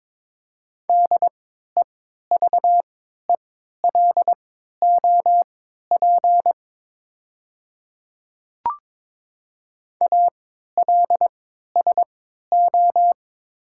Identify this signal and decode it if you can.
Morse code, heard as a keyed tone, about 22 wpm, spaced as for 13 wpm, 700 Hz. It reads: DEVELOP ALSO
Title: 3x Character Spacing - Top 500 Words - 22wpm